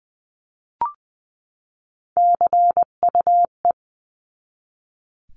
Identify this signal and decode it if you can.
Morse code, heard as a keyed tone, about 20 wpm, 700 Hz. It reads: CUE
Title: 259 Lesson 42d - Words 18-20wpm